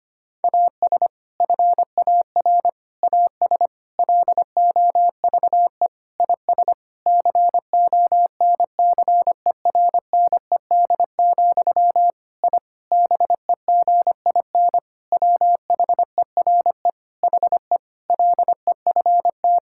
Morse code, heard as a keyed tone, about 25 wpm, 700 Hz. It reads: AS FAR AS LOVE IS CONCERNED, I BEGIN WHERE HE LEFT
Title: Sentences from Top 1000 Words - Review Part 6 - 25wpm